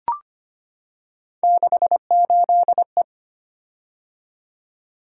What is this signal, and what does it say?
Morse code, heard as a keyed tone, about 25 wpm, 700 Hz. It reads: 68E
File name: Three Letter-Number Combinations - 25wpm